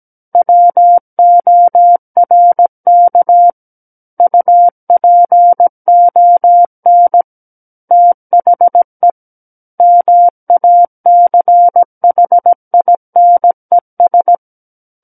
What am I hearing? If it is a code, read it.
Morse code, heard as a keyed tone, about 17 wpm, 700 Hz. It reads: WORK UPON THE MACHINES
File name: Book One / 08-Chapter 8 - 17 WPM